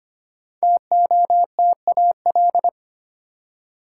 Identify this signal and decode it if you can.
Morse code, heard as a keyed tone, about 25 wpm, 700 Hz. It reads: TOTAL
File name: 5 Letter Words - First 3 Letters Given - 25wpm